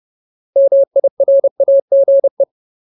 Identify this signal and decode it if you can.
Morse code, heard as a keyed tone, about 30 wpm, 550 Hz. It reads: MIRAGE
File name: Ultimate Word List / 30WPM Ultimate